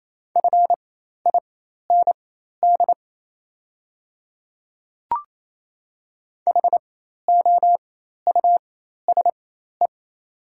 Morse code, heard as a keyed tone, about 27 wpm, spaced as for 13 wpm, 690 Hz. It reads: FIND HOUSE
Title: 4x Character Spacing - Top 200 Words - 28wpm